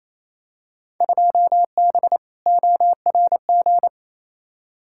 Morse code, heard as a keyed tone, about 28 wpm, 700 Hz. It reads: 2B ORG
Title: Field Day Exchanges Encore - 28wpm